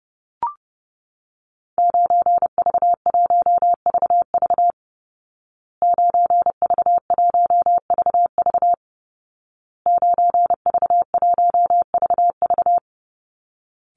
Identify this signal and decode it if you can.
Morse code, heard as a keyed tone, about 28 wpm, 700 Hz. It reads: 9V1VV 9V1VV 9V1VV
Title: Call Signs Repeated 2x - 30wpm